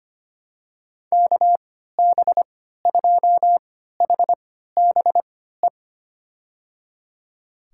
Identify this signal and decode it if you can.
Morse code, heard as a keyed tone, about 25 wpm, spaced as for 14 wpm, 700 Hz. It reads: KB2HBE
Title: USA Call Signs - 6 Characters - 3x Character Spacing - 25wpm